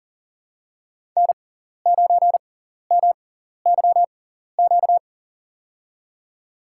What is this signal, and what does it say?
Morse code, heard as a keyed tone, about 40 wpm, spaced as for 14 wpm, 700 Hz. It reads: N9MYQ
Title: USA Call Signs - 5 Characters - 6x Character Spacing - 40wpm